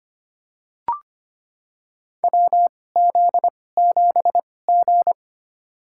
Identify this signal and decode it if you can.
Morse code, heard as a keyed tone, about 25 wpm, spaced as for 18 wpm, 700 Hz. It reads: WZ7G
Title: USA Call Signs - 4 Characters - 2x Character Spacing - 25wpm